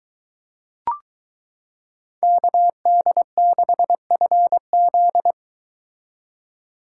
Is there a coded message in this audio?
Yes: Morse code from 2.2 to 5.3 s, heard as a keyed tone, about 22 wpm, 700 Hz. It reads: KD6FZ